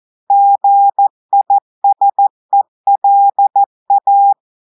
Morse code, heard as a keyed tone, about 14 wpm, 800 Hz. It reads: GISELA
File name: audios / portugais / portugais texte en clair 1 vitesse 14 mots minutes